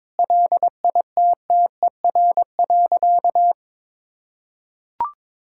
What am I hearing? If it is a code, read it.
Morse code, heard as a keyed tone, about 22 wpm, 700 Hz. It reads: LITTER.
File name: practice / 22wpm / Easy Sentences with Hints - 22wpm